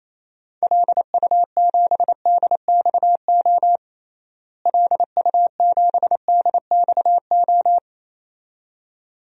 Morse code, heard as a keyed tone, about 28 wpm, 700 Hz. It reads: LU7DXO LU7DXO
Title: DXCC-100 Call Signs - 6 Characters - Repeated 2x - 28wpm